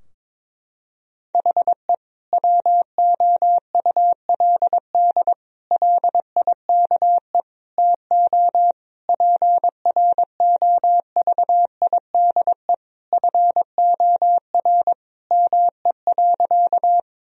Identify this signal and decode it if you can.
Morse code, heard as a keyed tone, about 21 wpm, 700 Hz. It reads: HE WOULD LIKE TO PROVIDE FOR ME.
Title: Sentences from Top 1000 Words - Review Part 3 - 22wpm